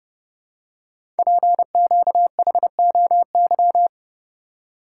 Morse code, heard as a keyed tone, about 30 wpm, 700 Hz. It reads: PQHOY
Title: Code Groups - 5 Letters - 30wpm